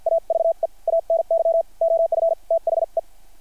Morse code, heard as a keyed tone, about 50 wpm, spaced as for 44 wpm, 660 Hz. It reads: AVE ANY OF THE
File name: archived / CRSnet